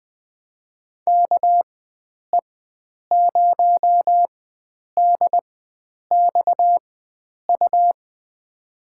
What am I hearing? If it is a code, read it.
Morse code, heard as a keyed tone, about 20 wpm, spaced as for 9 wpm, 700 Hz. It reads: KE0DXU
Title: USA Call Signs - 6 Characters - 4x Character Spacing - 20wpm